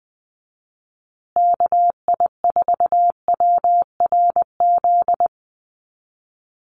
Morse code, heard as a keyed tone, about 20 wpm, 700 Hz. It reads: KI4WRZ
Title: USA Call Signs Repeated 3x 20wpm